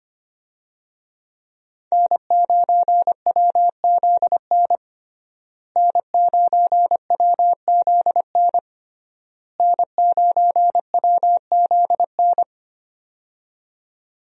Morse code, heard as a keyed tone, about 25 wpm, 700 Hz. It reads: N9WZN N9WZN N9WZN
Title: USA Call Signs - 5 Characters - Repeated 3x - 25wpm